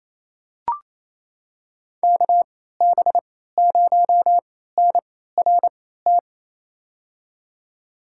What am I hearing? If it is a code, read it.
Morse code, heard as a keyed tone, about 28 wpm, spaced as for 16 wpm, 700 Hz. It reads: KB0NRT
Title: USA Call Signs - 6 Characters - 3x Character Spacing - 28wpm